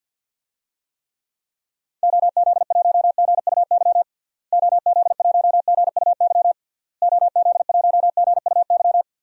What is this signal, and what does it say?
Morse code, heard as a keyed tone, about 51 wpm, 700 Hz. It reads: OZ1GUY OZ1GUY OZ1GUY
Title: DXCC-100 Call Signs - 6 Characters - Repeated 3x - 50wpm